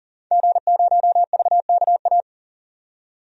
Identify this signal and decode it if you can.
Morse code, heard as a keyed tone, about 40 wpm, 700 Hz. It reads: G0VKA
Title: DXCC-100 Call Signs - 5 Characters - 40wpm